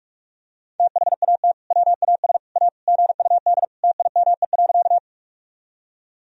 Morse code, heard as a keyed tone, about 46 wpm, 700 Hz. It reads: THAT WAS A GUD TIME.
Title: Sentences from Top 200 Words and Abbreviations 45wpm